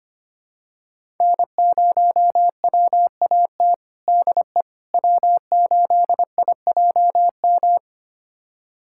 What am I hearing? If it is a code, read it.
Morse code, heard as a keyed tone, about 25 wpm, 700 Hz. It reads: N0WAT DE W8IJM
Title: USA Callsign DE Callsign - 5 Characters - 25wpm